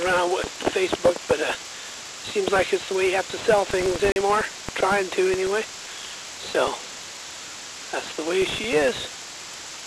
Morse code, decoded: none

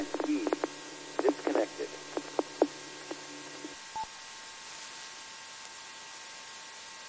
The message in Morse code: none

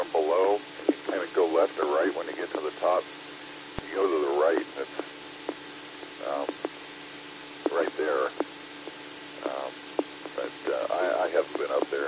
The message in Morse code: none

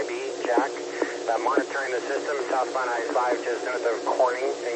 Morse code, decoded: none

